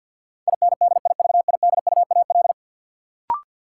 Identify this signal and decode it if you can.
Morse code, heard as a keyed tone, about 53 wpm, 700 Hz. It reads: INDIVIDUAL
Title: Sentences from Top 1000 Words - Review Part 4 - New Word Drill - 50wpm